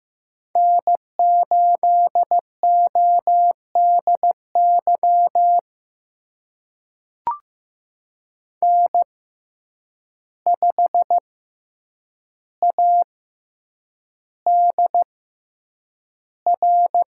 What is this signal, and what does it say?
Morse code, heard as a keyed tone, about 15 wpm, spaced as for 8 wpm, 700 Hz. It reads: N8ODY N5ADR
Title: USA Call Signs - 5 Characters - 6x Character Spacing - 15wpm